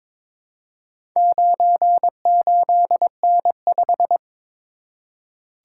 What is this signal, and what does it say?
Morse code, heard as a keyed tone, about 22 wpm, 690 Hz. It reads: 98N5
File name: practice / 22wpm / Four Letter-Number Combinations - 22wpm